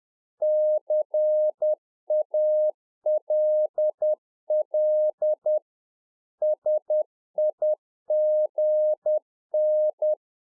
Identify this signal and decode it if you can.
Morse code, heard as a keyed tone, about 10 wpm, 610 Hz. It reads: CALL SIGN
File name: 2011-04-mp3 / RICK JJ0MPI 10WPM